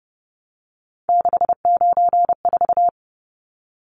Morse code, heard as a keyed tone, about 30 wpm, 700 Hz. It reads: 694
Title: Numbers - 3-Digit 30wpm